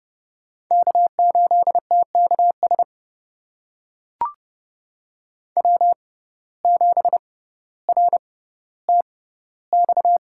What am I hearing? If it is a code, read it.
Morse code, heard as a keyed tone, about 30 wpm, spaced as for 15 wpm, 700 Hz. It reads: K8TKS W7RTX
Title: USA Call Signs - 6x Character Spacing - 30wpm